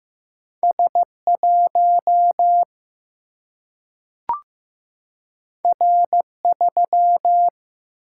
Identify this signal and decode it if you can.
Morse code, heard as a keyed tone, about 15 wpm, 700 Hz. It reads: S1 R3
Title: Single-Letter Single-Number 15wpm